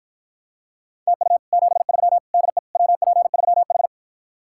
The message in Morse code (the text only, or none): TU 73 DE WP3H